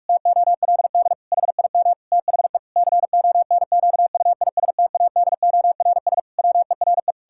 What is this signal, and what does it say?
Morse code, heard as a keyed tone, about 46 wpm, 700 Hz. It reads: TOLD HIM THE CONQUISTADORS WERE
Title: Pangram Challenge v2 - 45wpm